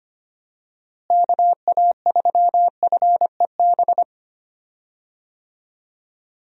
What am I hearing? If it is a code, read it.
Morse code, heard as a keyed tone, about 25 wpm, 700 Hz. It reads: KA3FEB
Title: USA Call Signs - 6 Characters - Repeated 3x - 25wpm